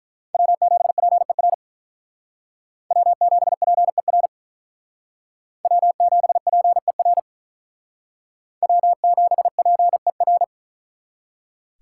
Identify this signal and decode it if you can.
Morse code, heard as a keyed tone, about 44 wpm, 700 Hz. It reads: W7PER W7PER W7PER W7PER
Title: Speed Racing - USA Call Signs 35wpm